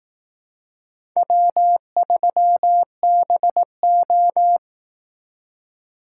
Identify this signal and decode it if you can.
Morse code, heard as a keyed tone, about 18 wpm, 700 Hz. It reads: W3BO
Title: Speed Racing - USA Call Signs 15wpm